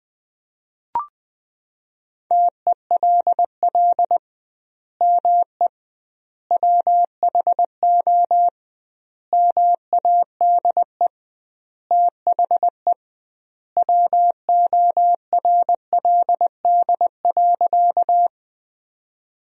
Morse code, heard as a keyed tone, about 20 wpm, 700 Hz. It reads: TELL ME WHO MADE THE WORLD.
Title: 2x Word-Spacing - Sentences from Top 200 Words - 20wpm